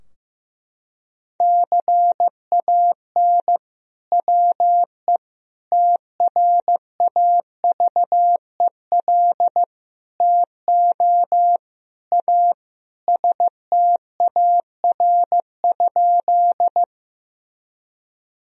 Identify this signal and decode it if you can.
Morse code, heard as a keyed tone, about 16 wpm, 700 Hz. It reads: CAN WE TRAVEL TO A STAR?
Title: Sentences from Top 1000 Words - Review Part 2 - 15wpm